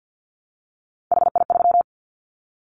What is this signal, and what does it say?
Morse code, heard as a keyed tone, about 50 wpm, 700 Hz. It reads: HI3
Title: Callsign Prefixes for Top 100 DXCC - 50wpm